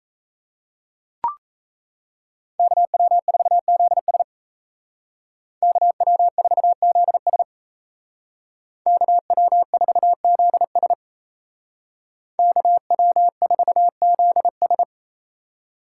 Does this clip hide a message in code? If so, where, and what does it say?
Yes: Morse code from 2.6 to 14.8 s, heard as a keyed tone, about 36 wpm, 700 Hz. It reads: KW4ZS KW4ZS KW4ZS KW4ZS